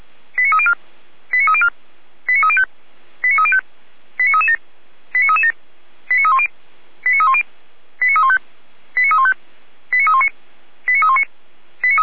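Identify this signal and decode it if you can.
Five-tone selective call (ZVEI), heard as quick runs of five tones, tones of about 70 ms. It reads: 89304 89304 89305 89305 89337 89337 89310 89310 89315 89315 89319 89319
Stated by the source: ZVEI five-tone call received by SDR, fire brigade Orlamünde (feuerwehr-orlamuende) recording.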